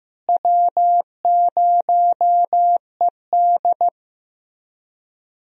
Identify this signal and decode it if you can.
Morse code, heard as a keyed tone, about 16 wpm, 700 Hz. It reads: W0ED
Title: USA Call Signs - 4 Characters - Repeated 3x - 15wpm